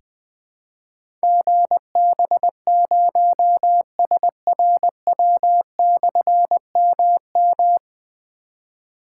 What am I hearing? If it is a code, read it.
Morse code, heard as a keyed tone, about 20 wpm, 700 Hz. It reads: GB0SRW/MM
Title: Difficult Call Signs 20wpm